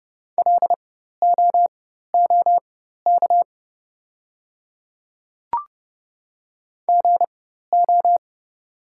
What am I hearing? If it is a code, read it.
Morse code, heard as a keyed tone, about 30 wpm, spaced as for 14 wpm, 690 Hz. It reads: LOOK GO
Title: 4x Character Spacing - Top 200 Words - 30wpm